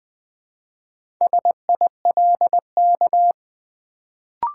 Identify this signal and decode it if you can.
Morse code, heard as a keyed tone, about 20 wpm, 700 Hz. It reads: SILK
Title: Build A Word - 1 Letter At A Time - 4-Letter Words - 20wpm